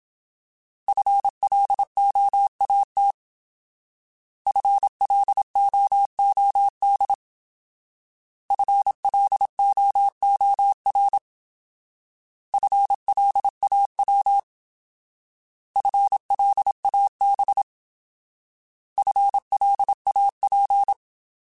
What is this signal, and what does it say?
Morse code, heard as a keyed tone, about 27 wpm, 800 Hz. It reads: FLOAT FLOOD FLOOR FLAW FLAB FLAP
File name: Alphabetical / Alphabetical mp3s / Fs @25